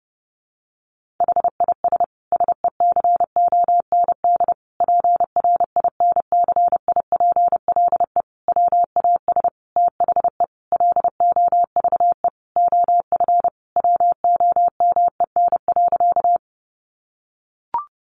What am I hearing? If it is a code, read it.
Morse code, heard as a keyed tone, about 30 wpm, 700 Hz. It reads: HIS SECOND PRINCIPLE WAS THE LOVE OF WOMEN.
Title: Sentences from Top 1400 Words 30wpm